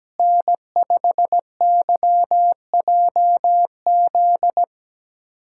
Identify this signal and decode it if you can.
Morse code, heard as a keyed tone, about 17 wpm, 700 Hz. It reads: N5YJZ
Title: USA Callsign DE Callsign - 5 Characters - 17wpm